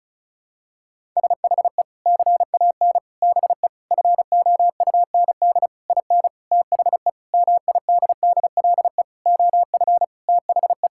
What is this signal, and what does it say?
Morse code, heard as a keyed tone, about 36 wpm, 700 Hz. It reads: SHE CAN BE FOUND IN THE MIDDLE OF THE
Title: Sentences from Top 1000 Words - Review Part 6 - 35wpm